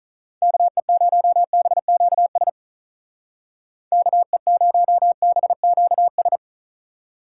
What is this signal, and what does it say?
Morse code, heard as a keyed tone, about 46 wpm, 700 Hz. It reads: KE0BQS KE0BQS